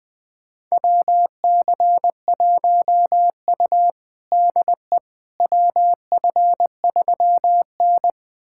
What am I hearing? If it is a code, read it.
Morse code, heard as a keyed tone, about 20 wpm, 700 Hz. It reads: WC1U DE WF3N
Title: USA Callsign DE Callsign - 4 Characters - 20wpm